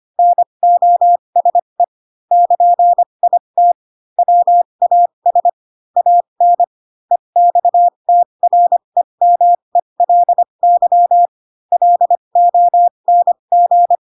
Morse code, heard as a keyed tone, about 25 wpm, spaced as for 22 wpm, 700 Hz. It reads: NOSE <KN>IT WAS AN EXTREMELY LONG
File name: Pinocchio / Pinocchio ch3 18 25 ik1hge